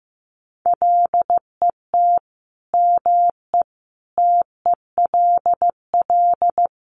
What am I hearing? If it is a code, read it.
Morse code, heard as a keyed tone, about 15 wpm, 700 Hz. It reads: LET ME TELL
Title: Sets of 5 Words Encore 15wpm